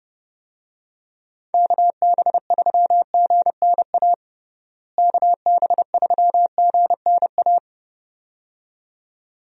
Morse code, heard as a keyed tone, about 32 wpm, 700 Hz. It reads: KB3GNA KB3GNA
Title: USA Call Signs - 6 Characters - Repeated 2x - 30wpm